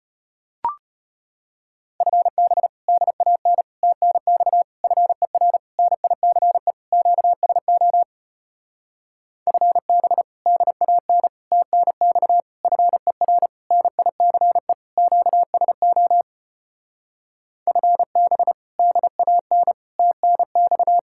Morse code, heard as a keyed tone, about 33 wpm, 700 Hz. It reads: FB DAN TNX FER NICE QSO FB DAN TNX FER NICE QSO FB DAN TNX